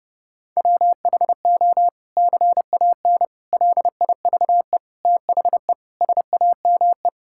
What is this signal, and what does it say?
Morse code, heard as a keyed tone, about 32 wpm, 700 Hz. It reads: WHO CAN LIVE THE SAME
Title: Sentences from Top 1600 Words 30wpm